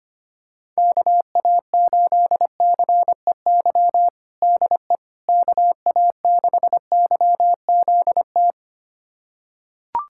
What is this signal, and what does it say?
Morse code, heard as a keyed tone, about 25 wpm, 700 Hz. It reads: KA8CEY DE KA6YZT
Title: USA Callsign DE Callsign - 6 Characters - 25wpm